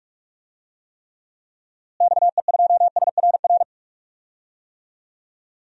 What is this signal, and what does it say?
Morse code, heard as a keyed tone, about 46 wpm, 700 Hz. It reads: XE2SRR
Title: DXCC-100 Call Signs - 6 Characters - Repeated 3x - 45wpm